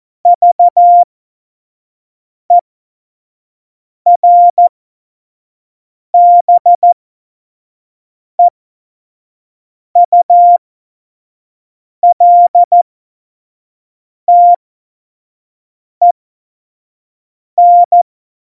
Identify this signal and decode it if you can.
Morse code, heard as a keyed tone, about 14 wpm, spaced as for 5 wpm, 700 Hz. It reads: VERBEULTEN
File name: Morsekurs / Witze / Witz 28 Juli 2020, 25 F70